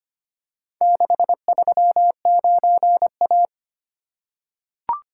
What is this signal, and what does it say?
Morse code, heard as a keyed tone, about 25 wpm, 690 Hz. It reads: 639A
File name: practice / 25wpm / Four Letter-Number Combinations - 25wpm